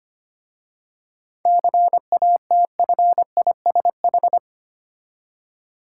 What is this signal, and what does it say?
Morse code, heard as a keyed tone, about 25 wpm, 700 Hz. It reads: CATFISH
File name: practice / 25wpm / Difficult Words with Hints - 25wpm